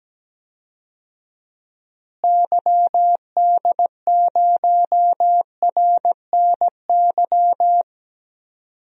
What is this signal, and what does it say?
Morse code, heard as a keyed tone, about 17 wpm, 700 Hz. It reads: YD0RNY
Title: DXCC-100 Call Signs - 6 Characters - Repeated 3x - 17wpm